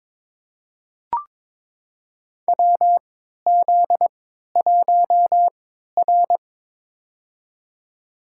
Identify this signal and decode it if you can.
Morse code, heard as a keyed tone, about 22 wpm, spaced as for 13 wpm, 700 Hz. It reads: WZ1R
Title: USA Call Signs - 4 Characters - 3x Character Spacing - 22wpm